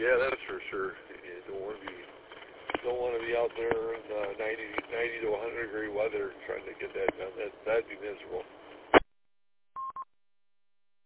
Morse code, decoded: none

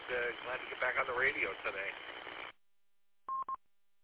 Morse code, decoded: none

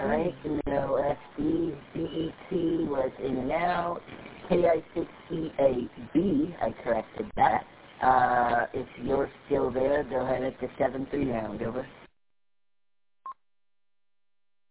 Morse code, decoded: none

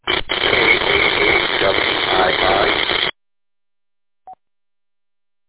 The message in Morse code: none

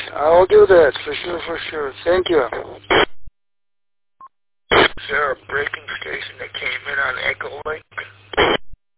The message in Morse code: none

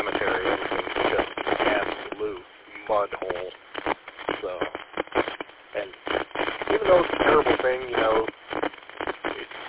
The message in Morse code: none

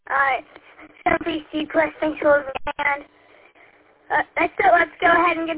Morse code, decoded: none

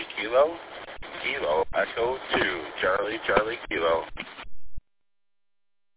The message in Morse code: none

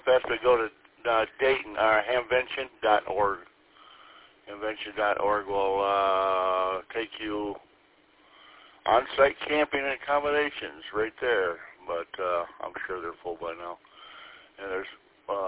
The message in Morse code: none

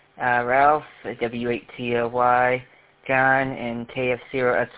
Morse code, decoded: none

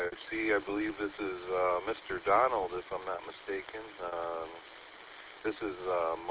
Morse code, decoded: none

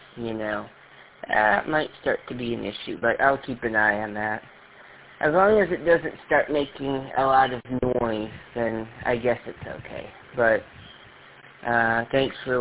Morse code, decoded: none

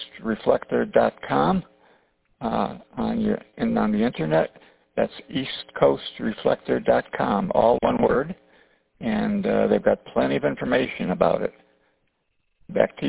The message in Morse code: none

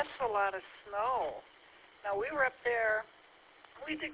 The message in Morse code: none

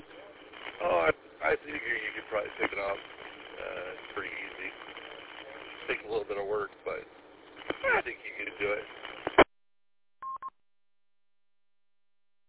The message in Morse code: none

